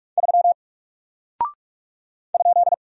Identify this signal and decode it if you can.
Morse code, heard as a keyed tone, about 44 wpm, 700 Hz. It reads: 3 ?
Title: Single Character (Letter:Number:Punctuation:Prosign) 45wpm